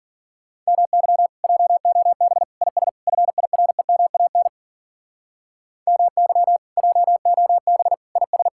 Morse code, heard as a keyed tone, about 52 wpm, 700 Hz. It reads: MY JOB IS FIREMAN MY JOB IS